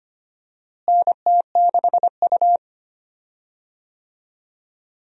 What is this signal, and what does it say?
Morse code, heard as a keyed tone, about 25 wpm, 700 Hz. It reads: NT6U
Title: USA Call Signs - 4 Characters - Repeated 2x - 25wpm